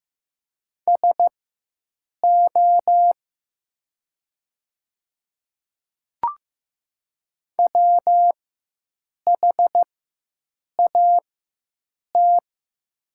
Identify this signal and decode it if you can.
Morse code, heard as a keyed tone, about 15 wpm, spaced as for 7 wpm, 700 Hz. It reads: SO WHAT
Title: 4x Character Spacing - Top 500 Words - 15wpm